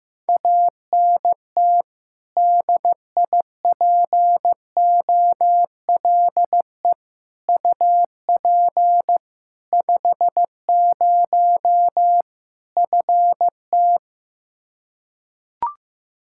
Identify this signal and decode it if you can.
Morse code, heard as a keyed tone, about 16 wpm, 700 Hz. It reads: ANT DIPOLE UP 50 FT
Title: K1USN SST QSO Practice 15wpm